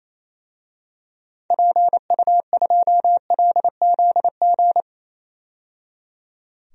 Morse code, heard as a keyed tone, about 27 wpm, 700 Hz. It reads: PU2LZG
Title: DXCC-100 Call Signs - 6 Characters - 28wpm